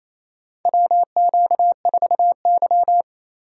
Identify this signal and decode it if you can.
Morse code, heard as a keyed tone, about 28 wpm, 700 Hz. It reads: WQ4Y
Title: USA Call Signs - 4 Characters - 3x Character Spacing - 28wpm